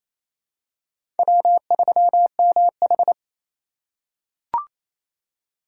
Morse code, heard as a keyed tone, about 28 wpm, 700 Hz. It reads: W3MH
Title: USA Call Signs - 4 Characters - 4x Character Spacing - 28wpm